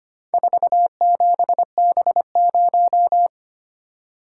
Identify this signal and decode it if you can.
Morse code, heard as a keyed tone, about 25 wpm, 700 Hz. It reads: 47B0
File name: practice / 25wpm / Four Letter-Number Combinations - 25wpm